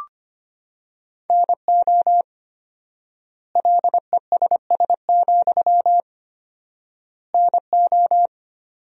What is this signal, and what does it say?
Morse code, heard as a keyed tone, about 25 wpm, 700 Hz. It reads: NO LESS, NO
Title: 4x Word-Spacing - Sentences from Top 400 Words - 25wpm